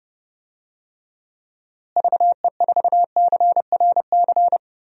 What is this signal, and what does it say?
Morse code, heard as a keyed tone, about 30 wpm, 700 Hz. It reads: VE4CRC
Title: DXCC-100 Call Signs - 6 Characters - Repeated 2x - 30wpm